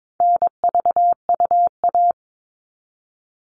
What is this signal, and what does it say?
Morse code, heard as a keyed tone, about 22 wpm, 700 Hz. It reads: NVUA
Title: All Cut Numbers - 4-Digit 22wpm